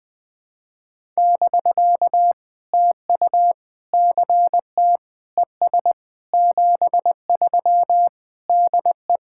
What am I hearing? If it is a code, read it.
Morse code, heard as a keyed tone, about 21 wpm, 700 Hz. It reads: <BK> TU CT ES 73 DE
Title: POTA QSO Practice - 20wpm